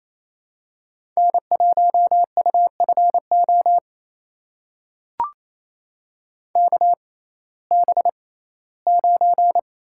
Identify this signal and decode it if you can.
Morse code, heard as a keyed tone, about 28 wpm, spaced as for 17 wpm, 700 Hz. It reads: N1UFO KB9
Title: USA Call Signs - 6x Character Spacing - 28wpm